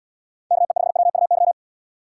Morse code, heard as a keyed tone, about 76 wpm, 700 Hz. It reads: KE5WUY